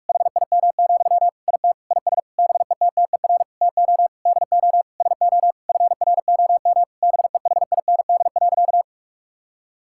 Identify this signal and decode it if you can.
Morse code, heard as a keyed tone, about 46 wpm, 700 Hz. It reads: HIM, IT IS BETTER TO DO SO FROM BEHIND.
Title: Sentences from Top 1600 Words 45wpm